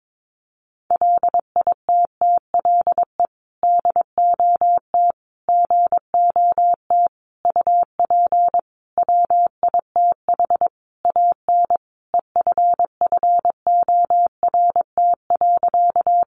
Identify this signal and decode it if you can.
Morse code, heard as a keyed tone, about 22 wpm, 700 Hz. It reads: LITTLE DOT GOT UP WITH AN EFFORT.
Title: Sentences from Top 1500 Words 22wpm